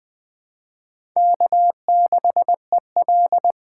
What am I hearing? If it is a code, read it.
Morse code, heard as a keyed tone, about 20 wpm, 700 Hz. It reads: K6EL
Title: 2023 SOTA Chaser Call Signs - 20wpm